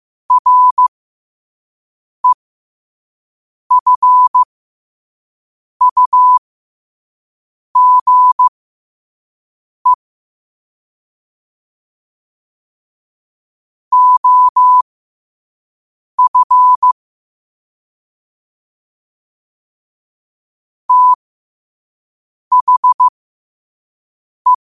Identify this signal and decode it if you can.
Morse code, heard as a keyed tone, about 15 wpm, spaced as for 5 wpm, 1000 Hz. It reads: REFUGE OF THE
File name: qotdfiles / 5wpm